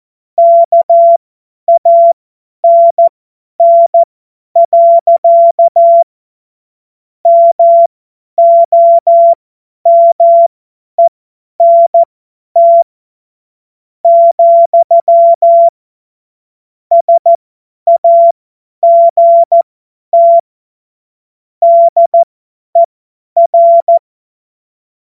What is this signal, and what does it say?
Morse code, heard as a keyed tone, about 14 wpm, spaced as for 10 wpm, 680 Hz. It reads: KANN. MOMENT, SAGT DER
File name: Morsekurs / Witze / Witz 21. Juni 2022 50 F70